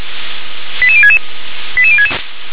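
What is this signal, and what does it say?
Five-tone selective call (ZVEI), heard as quick runs of five tones, tones of about 70 ms. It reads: 88066 88066
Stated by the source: ZVEI five-tone selective call received on an SDR.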